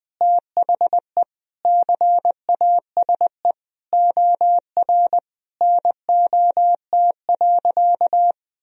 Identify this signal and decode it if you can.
Morse code, heard as a keyed tone, about 20 wpm, 700 Hz. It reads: THE CASE OR NOT.
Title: Sentences from Top 1000 Words - Review Part 3 - 20wpm